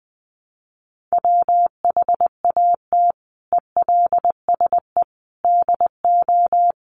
Morse code, heard as a keyed tone, about 20 wpm, 700 Hz. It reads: WHAT ELSE DO
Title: Sentences from Top 1100 Words 20wpm